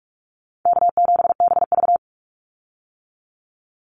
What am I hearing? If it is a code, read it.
Morse code, heard as a keyed tone, about 44 wpm, 700 Hz. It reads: K7BV
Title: Call Signs - 45wpm